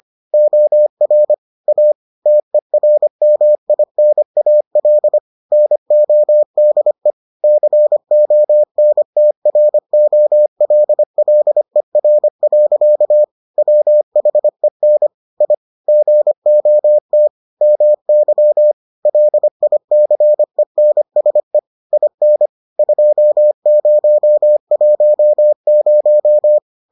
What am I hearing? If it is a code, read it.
Morse code, heard as a keyed tone, about 25 wpm, 600 Hz. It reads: OR A TERMINAL NODE CONTROLLER. WHEN I GOT MY LICENSE IN 2010